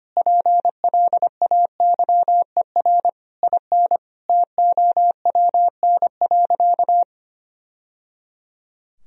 Morse code, heard as a keyed tone, about 25 wpm, 700 Hz. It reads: PLAYER IN TOWN.
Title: Sentences from Top 1000 Words - Review Part 4 - 25wpm